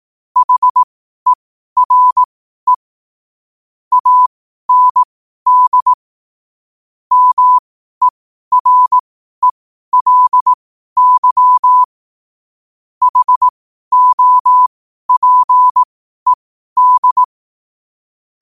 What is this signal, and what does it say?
Morse code, heard as a keyed tone, about 18 wpm, spaced as for 12 wpm, 1000 Hz. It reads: HERE AND MERELY HOPED
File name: qotdfiles / 12wpm